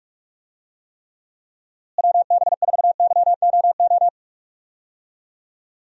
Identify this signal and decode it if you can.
Morse code, heard as a keyed tone, about 45 wpm, 700 Hz. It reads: WB4YOO
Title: USA Call Signs - 6 Characters - Repeated 3x - 45wpm